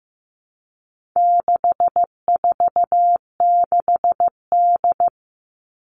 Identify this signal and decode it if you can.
Morse code, heard as a keyed tone, about 15 wpm, 700 Hz. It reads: 646D
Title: All Cut Numbers - 4-Digit 15wpm